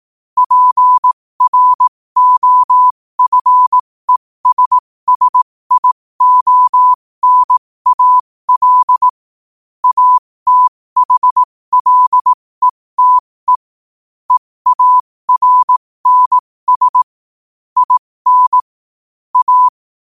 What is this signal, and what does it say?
Morse code, heard as a keyed tone, about 18 wpm, spaced as for 15 wpm, 1000 Hz. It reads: PROFESSIONAL ATHLETE EARNS IN A
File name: qotdfiles / 15wpm